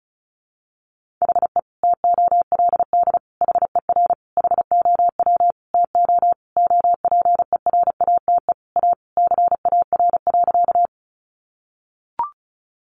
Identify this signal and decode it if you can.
Morse code, heard as a keyed tone, about 34 wpm, 700 Hz. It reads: HE TOLD HER HOW TO OPERATE A CAR.
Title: Sentences from Top 1500 Words 35wpm